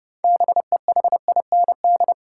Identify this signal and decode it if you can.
Morse code, heard as a keyed tone, about 30 wpm, 700 Hz. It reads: BEHIND
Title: Sentences from Top 500 Words - New Word Drill - 30wpm